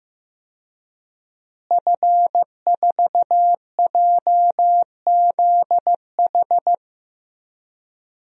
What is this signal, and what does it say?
Morse code, heard as a keyed tone, about 15 wpm, 700 Hz. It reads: F4JZH
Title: DXCC-100 Call Signs - 5 Characters - Repeated 2x - 15wpm